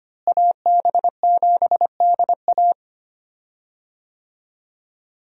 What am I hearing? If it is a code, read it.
Morse code, heard as a keyed tone, about 25 wpm, 700 Hz. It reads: AB7DA
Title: USA Call Signs - 5 Characters - Repeated 3x - 25wpm